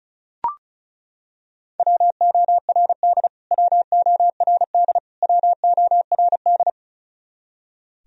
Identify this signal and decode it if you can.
Morse code, heard as a keyed tone, about 34 wpm, 700 Hz. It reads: WORD WORD WORD
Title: Top 100 Words Repeated 35wpm